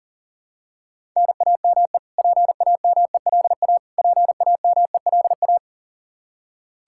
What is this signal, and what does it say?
Morse code, heard as a keyed tone, about 40 wpm, 700 Hz. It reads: NAME PAMELA PAMELA
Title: Practice QSO - Exchange 1 - Beginning - 40wpm